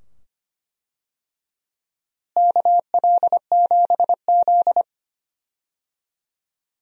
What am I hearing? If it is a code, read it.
Morse code, heard as a keyed tone, about 25 wpm, 700 Hz. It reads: KL7Z
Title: DXCC-100 Call Signs - 4 Characters - Repeated 2x - 25wpm